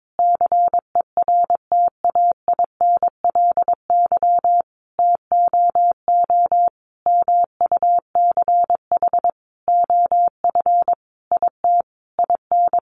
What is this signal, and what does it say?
Morse code, heard as a keyed tone, about 22 wpm, 700 Hz. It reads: CERTAINLY TOO MUCH OF IT IN
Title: Sentences from Top 900 Words 22wpm